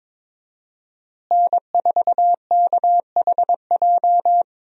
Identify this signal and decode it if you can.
Morse code, heard as a keyed tone, about 22 wpm, 700 Hz. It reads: N4KHJ
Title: USA Call Signs - 5 Characters - Repeated 2x - 22wpm